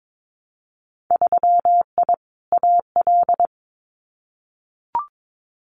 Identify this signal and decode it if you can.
Morse code, heard as a keyed tone, about 22 wpm, 700 Hz. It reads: 3I AL
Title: Winter Field Day Exchanges Encore 22wpm